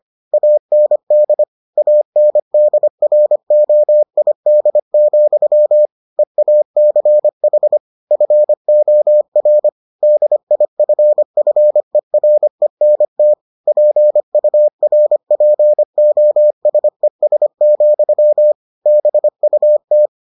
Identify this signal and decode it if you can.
Morse code, heard as a keyed tone, about 25 wpm, 600 Hz. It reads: AND ANDROID, EACH FOR DIFFERENT PURPOSES, BUT